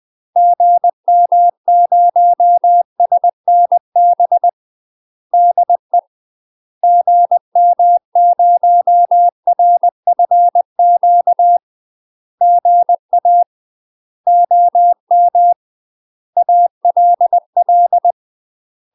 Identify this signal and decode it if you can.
Morse code, heard as a keyed tone, about 20 wpm, 700 Hz. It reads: GM0SNB DE GM0RFQ GA OM ALL
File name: UK MASTER QSOs TRAINING / 20WPM / UK MASTER QSOs 0060